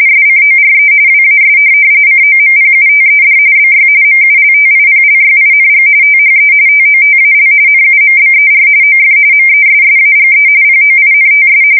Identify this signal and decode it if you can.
RTTY, about 45.45 baud. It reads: SCORE WAS COMPUTED BY MULTIPLYING THE TOTAL NUMBER OF MESSAGE POINTS
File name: fsk170 / teletype-RTTY